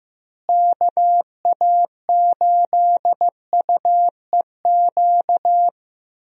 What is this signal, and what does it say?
Morse code, heard as a keyed tone, about 15 wpm, 700 Hz. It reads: KA8UEQ
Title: USA Call Signs - 6 Characters - Repeated 3x - 15wpm